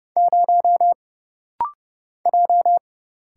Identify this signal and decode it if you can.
Morse code, heard as a keyed tone, about 30 wpm, 700 Hz. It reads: 0 J
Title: Single Letter-Number - Rapid-Fire 30wpm